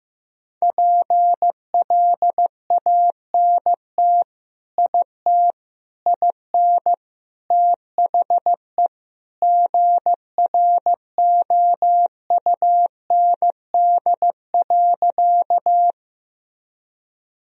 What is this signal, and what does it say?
Morse code, heard as a keyed tone, about 15 wpm, 700 Hz. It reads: PLANT IT IN THE GROUND.
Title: Sentences from Top 1000 Words - Review Part 1 - 15wpm